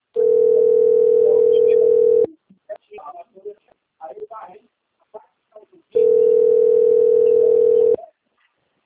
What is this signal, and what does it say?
Ringback tone (North America), 440 + 480 Hz, with speech.
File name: rg-1600-unknown- - - .68978